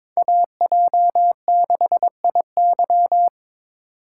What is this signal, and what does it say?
Morse code, heard as a keyed tone, about 22 wpm, 700 Hz. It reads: AJ6IY
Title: USA Call Signs - 5 Characters - 4x Character Spacing - 22wpm